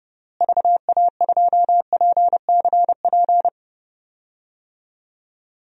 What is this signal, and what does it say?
Morse code, heard as a keyed tone, about 30 wpm, 700 Hz. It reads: VA2PCP
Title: DXCC-100 Call Signs - 6 Characters - Repeated 3x - 30wpm